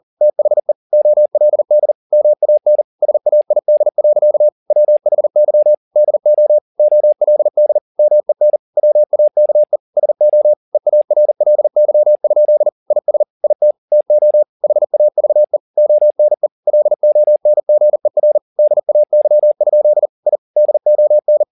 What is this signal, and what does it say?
Morse code, heard as a keyed tone, about 40 wpm, 600 Hz. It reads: THE OLD MAN SAID. WHY DO OLD MEN WAKE SO EARLY? IS IT TO HAVE ONE LONGER DAY? I DON